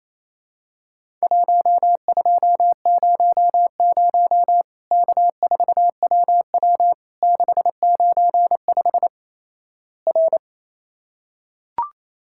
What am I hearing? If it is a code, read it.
Morse code, heard as a keyed tone, about 28 wpm, 700 Hz. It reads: 1200 K4WW 695 R
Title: Worked All Europe DX Contest - QTC Exchange - 28wpm